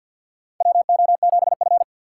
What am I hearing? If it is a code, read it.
Morse code, heard as a keyed tone, about 48 wpm, 700 Hz. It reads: WO7F
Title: USA Call Signs - 4 Characters - 2x Character Spacing - 50wpm